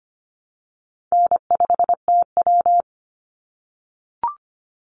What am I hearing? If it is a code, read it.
Morse code, heard as a keyed tone, about 25 wpm, 700 Hz. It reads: N5TW
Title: Easy Call Signs - 25wpm